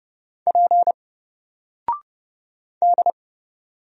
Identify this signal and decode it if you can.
Morse code, heard as a keyed tone, about 30 wpm, 700 Hz. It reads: P D
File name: Single-Letter-Number-30wpm